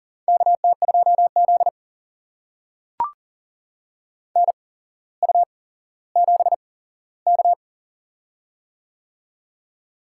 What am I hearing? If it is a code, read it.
Morse code, heard as a keyed tone, about 40 wpm, spaced as for 17 wpm, 700 Hz. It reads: KT2Z NU7K
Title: USA Call Signs - 4 Characters - 8x Character Spacing - 40wpm